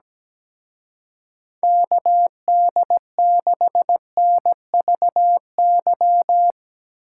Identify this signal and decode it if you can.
Morse code, heard as a keyed tone, about 19 wpm, 700 Hz. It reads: KD6NVY
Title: Speed Racing - USA 2x3 Call Signs 17wpm